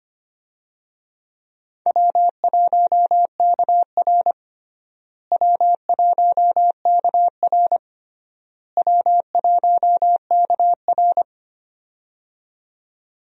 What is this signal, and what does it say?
Morse code, heard as a keyed tone, about 25 wpm, 700 Hz. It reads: W1KR W1KR W1KR
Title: USA Call Signs - 4 Characters - Repeated 3x - 25wpm